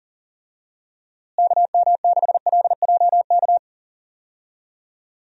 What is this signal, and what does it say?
Morse code, heard as a keyed tone, about 40 wpm, 700 Hz. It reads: KM6LJK
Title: USA Call Signs - 6 Characters - Repeated 2x - 40wpm